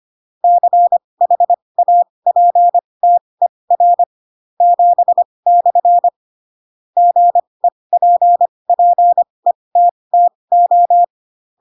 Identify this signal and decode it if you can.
Morse code, heard as a keyed tone, about 25 wpm, spaced as for 20 wpm, 700 Hz. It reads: CHAPTER 7/ GEPPETTO
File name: Pinocchio / Pinocchio ch7 15 25 ik1hge